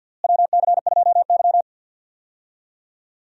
Morse code, heard as a keyed tone, about 53 wpm, 700 Hz. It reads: WC2Y
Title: USA Call Signs - 4 Characters - Repeated 2x - 50wpm